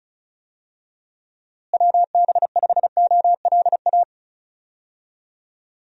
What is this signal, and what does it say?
Morse code, heard as a keyed tone, about 34 wpm, 700 Hz. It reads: WB5OLA
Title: USA Call Signs - 6 Characters - Repeated 2x - 35wpm